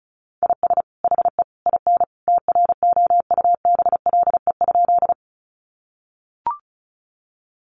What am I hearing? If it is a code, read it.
Morse code, heard as a keyed tone, about 34 wpm, 700 Hz. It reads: IS HE IN TROUBLE?
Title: Sentences from Top 1200 Words 35wpm